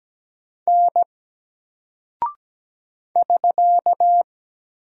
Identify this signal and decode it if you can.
Morse code, heard as a keyed tone, about 17 wpm, 700 Hz. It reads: N <SK>
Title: Single Character (Letter:Number:Punctuation:Prosign) 17wpm